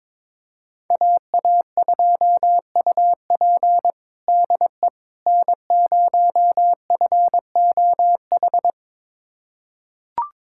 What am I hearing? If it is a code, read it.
Morse code, heard as a keyed tone, about 22 wpm, 700 Hz. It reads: AA2UP DE N0FOH
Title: USA Callsign DE Callsign - 5 Characters - 22wpm